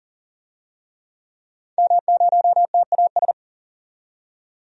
Morse code, heard as a keyed tone, about 38 wpm, 700 Hz. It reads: M0TAS
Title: DXCC-100 Call Signs - 5 Characters - 40wpm